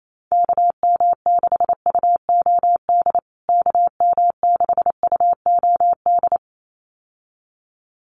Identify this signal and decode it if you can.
Morse code, heard as a keyed tone, about 28 wpm, 700 Hz. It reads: KM6UOD KM6UOD
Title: USA Call Signs Repeated 2x 28wpm